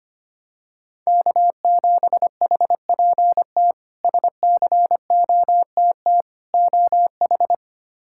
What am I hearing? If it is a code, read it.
Morse code, heard as a keyed tone, about 25 wpm, 700 Hz. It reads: K7HPT SCOTT OH
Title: K1USN SST QSO Practice - 25wpm